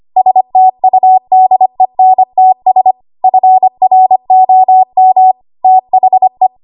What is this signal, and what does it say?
Morse code, heard as a keyed tone, about 25 wpm, 750 Hz. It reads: STUDENTS FROM THE
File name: Archive / 25 WPM / 25WPM